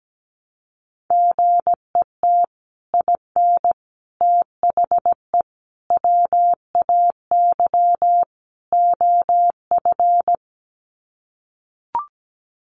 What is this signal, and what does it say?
Morse code, heard as a keyed tone, about 17 wpm, 700 Hz. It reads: GET IN THE WAY OF
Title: Sets of 5 Words Encore 17wpm